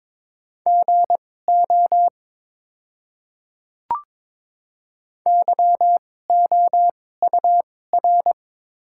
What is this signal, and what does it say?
Morse code, heard as a keyed tone, about 22 wpm, spaced as for 16 wpm, 700 Hz. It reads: GO YOUR
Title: 2x Character Spacing - Top 100 Words - 22wpm